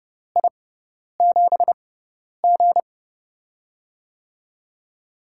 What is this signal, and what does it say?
Morse code, heard as a keyed tone, about 30 wpm, spaced as for 10 wpm, 700 Hz. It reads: I7G
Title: USA Call Signs - 4 Characters - 6x Character Spacing - 30wpm